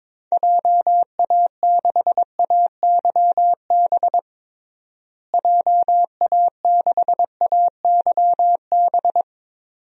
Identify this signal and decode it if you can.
Morse code, heard as a keyed tone, about 22 wpm, 700 Hz. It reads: JA6AYB JA6AYB
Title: DXCC-100 Call Signs - 6 Characters - Repeated 2x - 22wpm